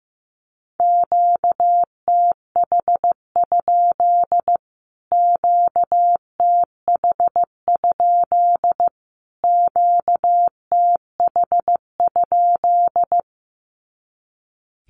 Morse code, heard as a keyed tone, about 15 wpm, 700 Hz. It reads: QTH? QTH? QTH?
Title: Q-Codes Repeated 3x 15wpm